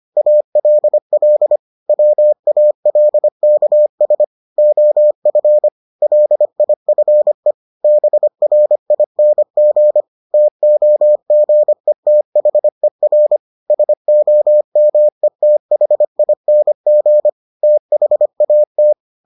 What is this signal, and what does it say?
Morse code, heard as a keyed tone, about 25 wpm, 600 Hz. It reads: ALL WALKS OF LIFE BRING TOGETHER SOMETHING THAT